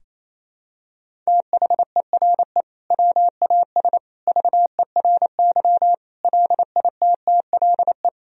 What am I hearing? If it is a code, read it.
Morse code, heard as a keyed tone, about 28 wpm, 700 Hz. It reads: THERE WAS VERY LITTLE
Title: Sentences from Top 1000 Words - Review Part 1 - 28wpm